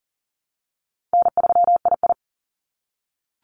Morse code, heard as a keyed tone, about 40 wpm, 700 Hz. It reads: N3II
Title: NA SOTA Activator Callsigns 40wpm